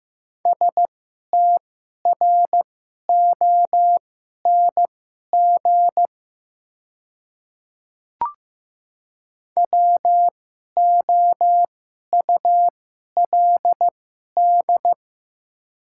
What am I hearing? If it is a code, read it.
Morse code, heard as a keyed tone, about 15 wpm, spaced as for 11 wpm, 690 Hz. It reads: STRONG WOULD
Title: 2x Character Spacing - Top 500 Words - 15wpm